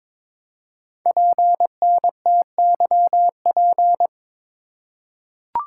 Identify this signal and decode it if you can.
Morse code, heard as a keyed tone, about 21 wpm, 700 Hz. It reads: PNTYP
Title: Code Groups - 5 Letter:Numbers - 22wpm